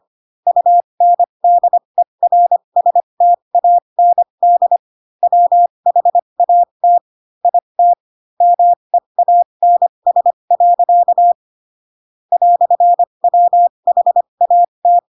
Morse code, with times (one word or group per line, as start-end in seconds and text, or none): 0.5-4.8 UNDERSTAND
5.2-7.0 WHAT
7.5-7.9 IT
8.4-11.3 MEANS.
12.3-15.0 "WHAT